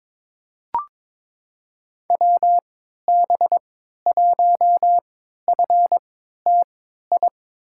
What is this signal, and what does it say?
Morse code, heard as a keyed tone, about 22 wpm, spaced as for 13 wpm, 700 Hz. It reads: WB1FTI
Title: USA Call Signs - 6 Characters - 3x Character Spacing - 22wpm